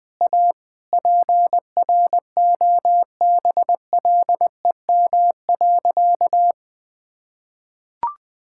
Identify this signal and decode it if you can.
Morse code, heard as a keyed tone, about 20 wpm, 700 Hz. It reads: A PROBLEM.
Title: Sentences with 9-Word Phrases - 20wpm